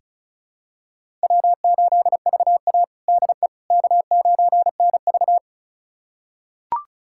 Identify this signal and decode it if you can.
Morse code, heard as a keyed tone, about 36 wpm, 700 Hz. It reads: W8VA DE K9NV
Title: USA Callsign DE Callsign - 4 Characters - 35wpm